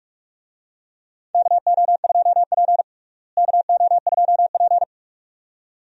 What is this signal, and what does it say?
Morse code, heard as a keyed tone, about 46 wpm, 700 Hz. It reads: KO2P KO2P
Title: USA Call Signs - 4 Characters - Repeated 2x - 45wpm